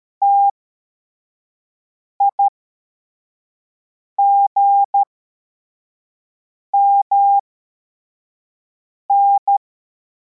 Morse code, heard as a keyed tone, about 13 wpm, spaced as for 4 wpm, 800 Hz. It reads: TIGMN
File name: formation / cw / Text61